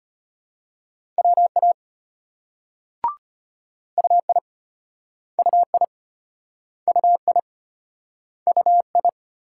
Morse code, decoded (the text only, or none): WA UI UI UI UI